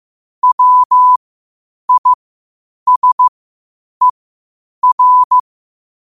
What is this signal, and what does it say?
Morse code, heard as a keyed tone, about 15 wpm, spaced as for 8 wpm, 1000 Hz. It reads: WISER
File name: qotdfiles / 8wpm